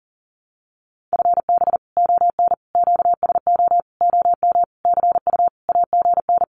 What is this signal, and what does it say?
Morse code, heard as a keyed tone, about 40 wpm, 700 Hz. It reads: FB ON QSO OM CU AGN
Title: Common QSO Phrases 40wpm